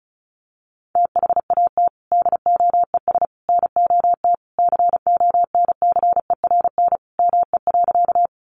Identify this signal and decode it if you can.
Morse code, heard as a keyed tone, about 34 wpm, 700 Hz. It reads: THAT DOES NOT CONCERN ME.
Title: Sentences from Top 1000 Words 35wpm